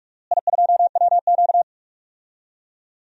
Morse code, heard as a keyed tone, about 42 wpm, 700 Hz. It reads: I2WQ
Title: DXCC-100 Call Signs - 4 Characters - Repeated 2x - 45wpm